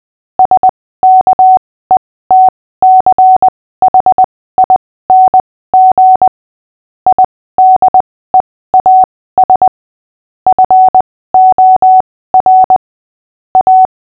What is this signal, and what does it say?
Morse code, heard as a keyed tone, about 20 wpm, spaced as for 15 wpm, 750 Hz. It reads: SKETCHING IDEAS FOR A
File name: cw / fists / Short-Story-6-15wpm